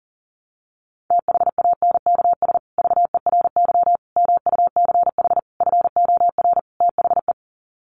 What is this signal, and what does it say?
Morse code, heard as a keyed tone, about 40 wpm, 700 Hz. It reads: THANKS VERY MUCH FOR THE
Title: Sets of 5 Words Encore2 40wpm